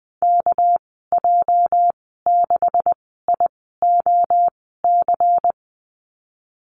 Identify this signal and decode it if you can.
Morse code, heard as a keyed tone, about 20 wpm, spaced as for 14 wpm, 700 Hz. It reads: KJ6IOC
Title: 245 Lesson 39g - Letter z - Call Signs 20wpm